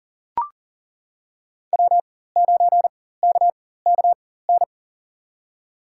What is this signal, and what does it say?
Morse code, heard as a keyed tone, about 38 wpm, spaced as for 18 wpm, 700 Hz. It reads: W9KKN
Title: USA Call Signs - 5 Characters - 4x Character Spacing - 40wpm